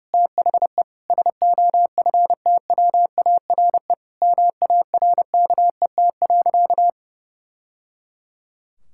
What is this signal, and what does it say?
Morse code, heard as a keyed tone, about 30 wpm, 700 Hz. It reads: THE SOFTWARE MARKET.
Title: Sentences from Top 1000 Words - Review Part 6 - 30wpm